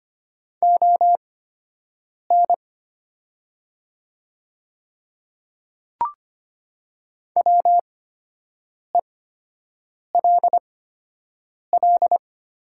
Morse code, heard as a keyed tone, about 25 wpm, spaced as for 7 wpm, 700 Hz. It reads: ON WELL
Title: 8x Character Spacing - Top 300 Words - 25wpm